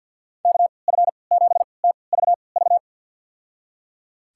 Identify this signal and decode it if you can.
Morse code, heard as a keyed tone, about 53 wpm, spaced as for 29 wpm, 700 Hz. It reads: KF7TVV
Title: USA Call Signs - 6 Characters - 3x Character Spacing - 50wpm